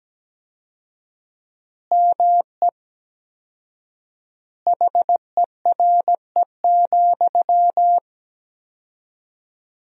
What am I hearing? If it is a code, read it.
Morse code, heard as a keyed tone, about 17 wpm, 700 Hz. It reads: ME HERE,
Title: 4x Word-Spacing - Sentences from Top 500 Words - 17wpm